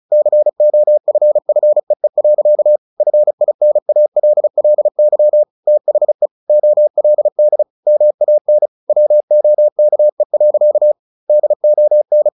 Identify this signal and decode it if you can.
Morse code, heard as a keyed tone, about 35 wpm, 600 Hz. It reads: COFFEE. FINALLY THE OLD MAN WOKE. DON
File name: the old man and the sea 2 35wpm